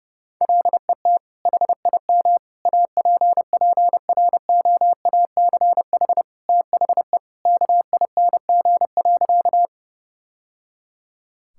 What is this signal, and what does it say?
Morse code, heard as a keyed tone, about 30 wpm, 700 Hz. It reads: LET HIM APPROACH THE KING.
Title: Sentences from Top 1000 Words - Review Part 6 - 30wpm